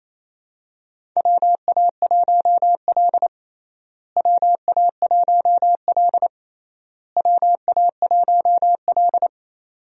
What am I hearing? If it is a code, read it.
Morse code, heard as a keyed tone, about 27 wpm, 700 Hz. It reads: WA1L WA1L WA1L
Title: USA Call Signs - 4 Characters - Repeated 3x - 28wpm